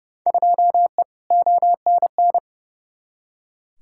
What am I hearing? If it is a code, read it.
Morse code, heard as a keyed tone, about 30 wpm, 700 Hz. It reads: 2E ONN
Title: Field Day Exchanges Encore - 30wpm